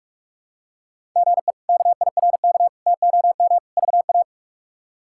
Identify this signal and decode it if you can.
Morse code, heard as a keyed tone, about 41 wpm, 700 Hz. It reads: GE KIRK TOM VA
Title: K1USN SST QSO Practice - 45wpm